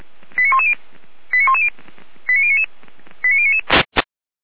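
Five-tone selective call (ZVEI), heard as quick runs of five tones, tones of about 70 ms. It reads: 89110 89110 89099 89099
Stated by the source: Five-tone ZVEI selective call received on an SDR.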